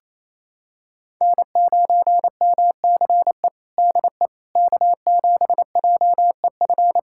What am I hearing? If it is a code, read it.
Morse code, heard as a keyed tone, about 27 wpm, 700 Hz. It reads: N9MCE DE K7JEF
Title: USA Callsign DE Callsign - 5 Characters - 28wpm